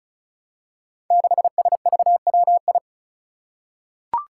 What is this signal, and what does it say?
Morse code, heard as a keyed tone, about 36 wpm, 700 Hz. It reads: 6SVWI
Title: Five Challenging Letter-Number Combinations 35wpm